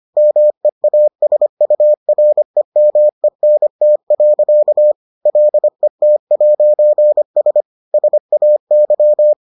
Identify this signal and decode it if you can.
Morse code, heard as a keyed tone, about 25 wpm, 600 Hz. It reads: MEASUREMENT. LET'S SAY